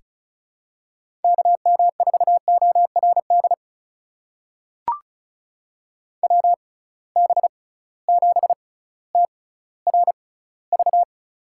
Morse code, heard as a keyed tone, about 36 wpm, spaced as for 18 wpm, 700 Hz. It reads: KM4ORD WB7TRV
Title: USA Call Signs - 6 Characters - 6x Character Spacing - 35wpm